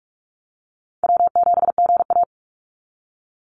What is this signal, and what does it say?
Morse code, heard as a keyed tone, about 46 wpm, 690 Hz. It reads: W7GA
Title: NA SOTA Chaser Callsigns 45wpm